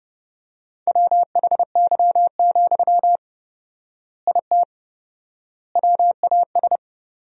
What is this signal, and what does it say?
Morse code, heard as a keyed tone, about 30 wpm, 700 Hz. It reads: WHY, IT WAS
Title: 4x Word-Spacing - Sentences from Top 300 Words - 30wpm